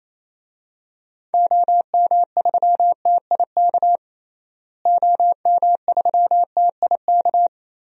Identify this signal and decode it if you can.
Morse code, heard as a keyed tone, about 27 wpm, 700 Hz. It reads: OM3TIK OM3TIK
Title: DXCC-100 Call Signs - 6 Characters - Repeated 3x - 28wpm